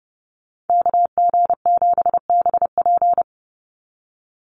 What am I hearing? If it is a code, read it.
Morse code, heard as a keyed tone, about 30 wpm, 700 Hz. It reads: KG7BP
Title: USA Call Signs 30wpm